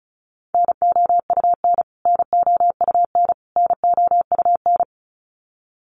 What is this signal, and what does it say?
Morse code, heard as a keyed tone, about 34 wpm, 700 Hz. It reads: NOUN NOUN NOUN
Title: Top 500 Words Repeated 35wpm